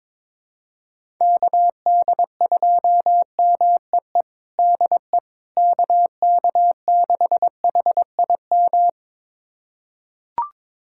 Morse code, heard as a keyed tone, about 21 wpm, 700 Hz. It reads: KD2MEE DE KK6HIM
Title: USA Callsign DE Callsign - 6 Characters - 22wpm